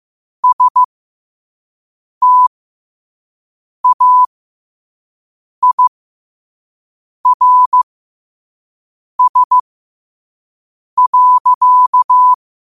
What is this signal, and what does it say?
Morse code, heard as a keyed tone, about 15 wpm, spaced as for 5 wpm, 1000 Hz. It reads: STAIRS.
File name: qotdfiles / 5wpm